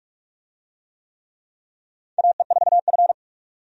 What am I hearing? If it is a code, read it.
Morse code, heard as a keyed tone, about 46 wpm, 700 Hz. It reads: AE4F
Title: USA Call Signs - 4 Characters - Repeated 3x - 45wpm